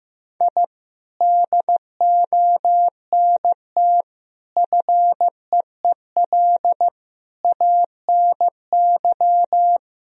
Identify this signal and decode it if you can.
Morse code, heard as a keyed tone, about 15 wpm, 700 Hz. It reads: I DONT FEEL ANY
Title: Sentences from Top 1000 Words - Review Part 3 - 15wpm